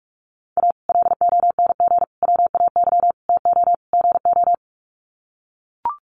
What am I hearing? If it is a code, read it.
Morse code, heard as a keyed tone, about 44 wpm, 700 Hz. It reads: A LONG WAY TO GO
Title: Sets of 5 Words 45wpm